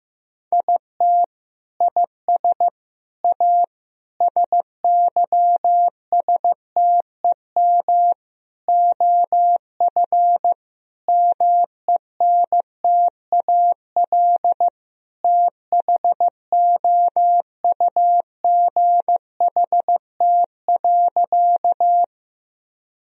Morse code, heard as a keyed tone, about 15 wpm, 700 Hz. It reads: IT IS A SYSTEM OF MENTAL THOUGHT.
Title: Sentences from Top 1000 Words - Review Part 4 - 15wpm